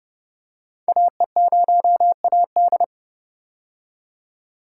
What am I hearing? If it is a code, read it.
Morse code, heard as a keyed tone, about 30 wpm, 700 Hz. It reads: AE0AD
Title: Code Groups - 5 Letter:Numbers - 30wpm